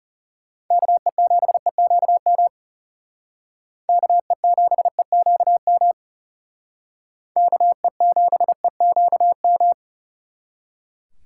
Morse code, heard as a keyed tone, about 36 wpm, 700 Hz. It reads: KE7EQM KE7EQM KE7EQM
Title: Speed Racing - USA 2x3 Call Signs 30wpm